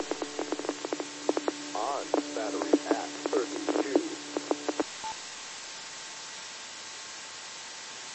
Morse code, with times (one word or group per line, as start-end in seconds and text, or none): none